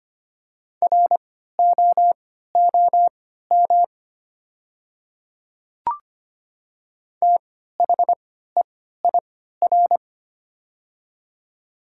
Morse code, heard as a keyed tone, about 25 wpm, spaced as for 14 wpm, 700 Hz. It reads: ROOM THEIR
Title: 3x Character Spacing - Top 300 Words - 25wpm